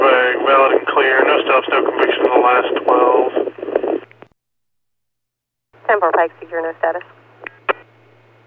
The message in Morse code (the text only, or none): none